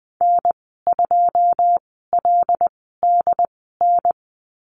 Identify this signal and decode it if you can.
Morse code, heard as a keyed tone, about 20 wpm, spaced as for 15 wpm, 700 Hz. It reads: N2LDN
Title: 088 Lesson 15g - Number 5 - Call Signs 20wpm